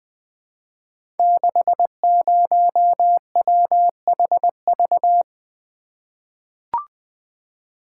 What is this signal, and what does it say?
Morse code, heard as a keyed tone, about 20 wpm, 700 Hz. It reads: 60WHV
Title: Code Groups - 5 Letter:Numbers - 20wpm